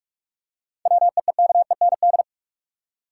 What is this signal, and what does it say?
Morse code, heard as a keyed tone, about 45 wpm, 700 Hz. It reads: WEEKEND
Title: Seven Letter Words - 45wpm